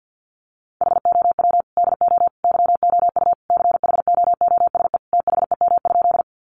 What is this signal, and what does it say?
Morse code, heard as a keyed tone, about 50 wpm, 700 Hz. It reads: HOW DO YOU CHOOSE THEM?
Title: Sentences from Top 700 Words 50wpm